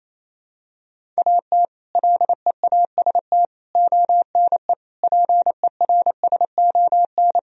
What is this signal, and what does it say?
Morse code, heard as a keyed tone, about 28 wpm, 700 Hz. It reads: AT LEAST ONE PERSON
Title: Sentences from Top 1000 Words - Review Part 1 - 28wpm